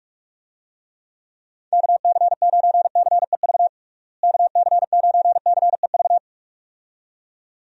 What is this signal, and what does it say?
Morse code, heard as a keyed tone, about 46 wpm, 700 Hz. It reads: KC9CEV KC9CEV
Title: USA Call Signs - 6 Characters - Repeated 2x - 45wpm